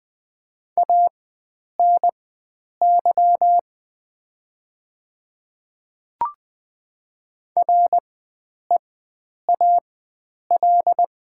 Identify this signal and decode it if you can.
Morse code, heard as a keyed tone, about 20 wpm, spaced as for 9 wpm, 700 Hz. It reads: ANY REAL
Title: 4x Character Spacing - Top 500 Words - 20wpm